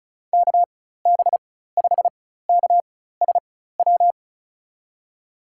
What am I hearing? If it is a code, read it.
Morse code, heard as a keyed tone, about 36 wpm, spaced as for 16 wpm, 700 Hz. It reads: KB5KSW
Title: USA Call Signs - 4x Character Spacing - 35wpm